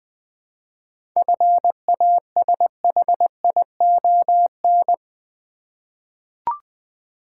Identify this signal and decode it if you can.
Morse code, heard as a keyed tone, about 20 wpm, 700 Hz. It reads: FASHION
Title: 7 Letter Words - First 3 Letters Given - 20wpm